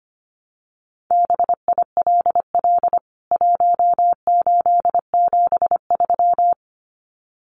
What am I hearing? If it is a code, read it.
Morse code, heard as a keyed tone, about 25 wpm, 700 Hz. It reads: BILL 1873
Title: CWT Exchange 25wpm